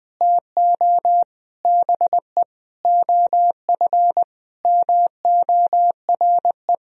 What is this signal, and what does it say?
Morse code, heard as a keyed tone, about 20 wpm, 700 Hz. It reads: TO BE OF MORE
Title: Sentences from Top 1000 Words - Review Part 2 - 20wpm